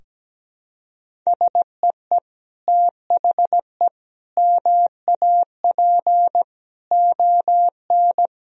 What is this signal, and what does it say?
Morse code, heard as a keyed tone, about 17 wpm, 700 Hz. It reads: SEE THE MAP ON
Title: Sentences from Top 1000 Words - Review Part 2 - 17wpm